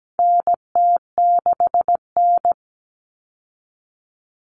Morse code, heard as a keyed tone, about 17 wpm, 700 Hz. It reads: NT6N
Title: All Cut Numbers - 4-Digit 17wpm